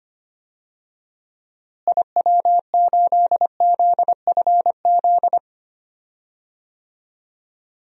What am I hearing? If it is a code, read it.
Morse code, heard as a keyed tone, about 25 wpm, 700 Hz. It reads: IW8ZFZ
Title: DXCC-100 Call Signs - 6 Characters - Repeated 3x - 25wpm